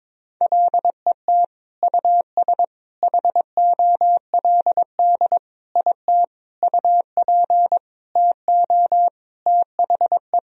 Morse code, heard as a keyed tone, about 22 wpm, 700 Hz. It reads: LET US HOLD IT UP TO THE
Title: Sentences from Top 1000 Words - Review Part 2 - 22wpm